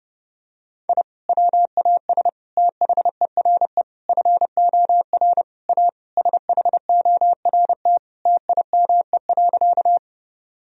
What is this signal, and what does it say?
Morse code, heard as a keyed tone, about 30 wpm, 700 Hz. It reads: I WAS THERE FOR A SHORT TIME.
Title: Sentences from Top 1000 Words - Review Part 2 - 30wpm